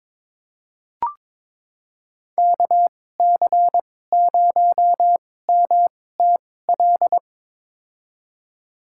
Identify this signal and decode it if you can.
Morse code, heard as a keyed tone, about 22 wpm, spaced as for 16 wpm, 700 Hz. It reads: KC0MTL
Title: USA Call Signs - 6 Characters - 2x Character Spacing - 22wpm